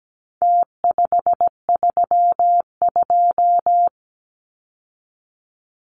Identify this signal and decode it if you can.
Morse code, heard as a keyed tone, about 17 wpm, 700 Hz. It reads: T532
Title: Numbers - Contest Style - 4-Digit 17wpm